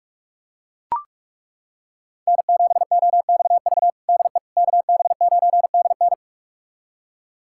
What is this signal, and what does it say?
Morse code, heard as a keyed tone, about 41 wpm, 700 Hz. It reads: N7OXV DE KB9DN
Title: USA Callsign DE Callsign - 5 Characters - 45wpm